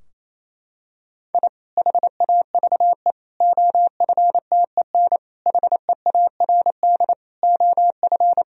Morse code, heard as a keyed tone, about 28 wpm, 700 Hz. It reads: I HAVE OFTEN HEARD OF
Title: Sentences from Top 1000 Words - Review Part 2 - 28wpm